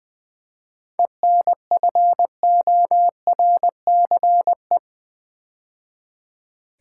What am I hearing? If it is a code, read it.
Morse code, heard as a keyed tone, about 21 wpm, 700 Hz. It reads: ENFORCE
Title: 7 Letter Words - First 3 Letters Given - 20wpm